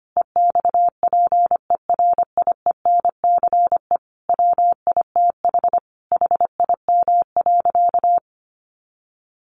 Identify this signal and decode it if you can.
Morse code, heard as a keyed tone, about 25 wpm, 700 Hz. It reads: EXPERIENCE WITH HIM.
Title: Sentences from Top 1400 Words 25wpm